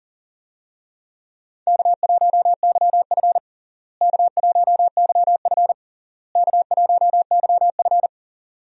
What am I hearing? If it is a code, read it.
Morse code, heard as a keyed tone, about 40 wpm, 700 Hz. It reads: K1YF K1YF K1YF
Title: USA Call Signs - 4 Characters - Repeated 3x - 40wpm